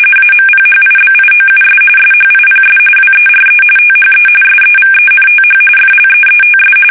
RTTY, about 45.45 baud. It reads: AMPHIBIOUS SHIPS WERE THE SEVENTH FLEETS
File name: fsk850 / battleshipnj